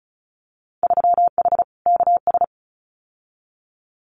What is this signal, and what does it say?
Morse code, heard as a keyed tone, about 36 wpm, 700 Hz. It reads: 3H KS
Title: Winter Field Day Exchanges Encore 35wpm